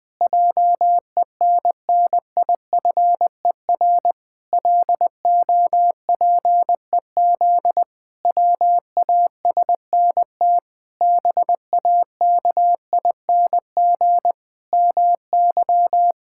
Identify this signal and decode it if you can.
Morse code, heard as a keyed tone, about 20 wpm, 700 Hz. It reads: JENNIFER LOPEZ WASNT BAKING MY
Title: Pangram Challenge v2 - 20wpm